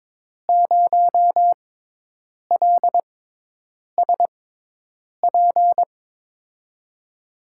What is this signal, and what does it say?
Morse code, heard as a keyed tone, about 22 wpm, spaced as for 8 wpm, 700 Hz. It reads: 0LSP
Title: USA Call Signs - 6x Character Spacing - 22wpm